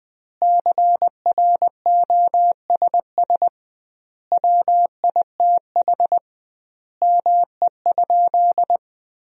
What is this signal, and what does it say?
Morse code, heard as a keyed tone, about 20 wpm, 700 Hz. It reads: CROSS WITH ME?
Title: 2x Word-Spacing - Sentences from Top 300 Words - 20wpm